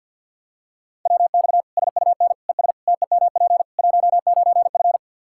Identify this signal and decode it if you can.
Morse code, heard as a keyed tone, about 52 wpm, 700 Hz. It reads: WX SUN ES TEMP 19F
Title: Practice QSO - Exchange 2 - Info - 50wpm